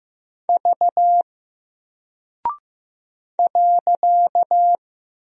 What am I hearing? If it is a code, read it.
Morse code, heard as a keyed tone, about 15 wpm, 700 Hz. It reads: V .
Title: Single Character (Letter:Number:Punctuation:Prosign) Alt 15wpm